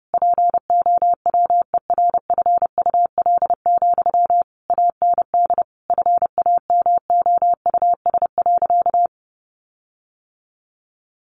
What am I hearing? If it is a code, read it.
Morse code, heard as a keyed tone, about 30 wpm, 700 Hz. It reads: POWERFUL, AND FAMOUS.
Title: Sentences from Top 1500 Words 30wpm